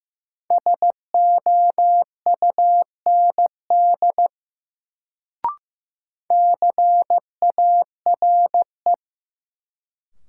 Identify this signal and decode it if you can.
Morse code, heard as a keyed tone, about 15 wpm, 700 Hz. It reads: SOUND CARE
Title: Sets of 3 Words - New Word Drill - 15wpm